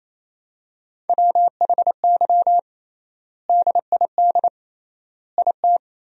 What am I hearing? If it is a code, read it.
Morse code, heard as a keyed tone, about 28 wpm, 700 Hz. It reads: WHY DID IT
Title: 3x Word-Spacing - Sentences from Top 400 Words - 28wpm